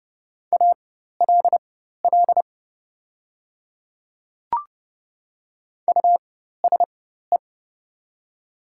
Morse code, heard as a keyed tone, about 30 wpm, spaced as for 14 wpm, 700 Hz. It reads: ALL USE
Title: 4x Character Spacing - Top 100 Words - 30wpm